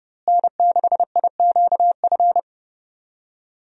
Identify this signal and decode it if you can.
Morse code, heard as a keyed tone, about 30 wpm, 700 Hz. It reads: N6IQF
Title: USA Call Signs - 5 Characters - Repeated 3x - 30wpm